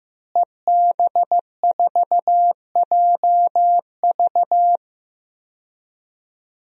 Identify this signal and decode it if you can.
Morse code, heard as a keyed tone, about 15 wpm, 700 Hz. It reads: EB4JV
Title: DXCC-100 Call Signs - 5 Characters - Repeated 3x - 15wpm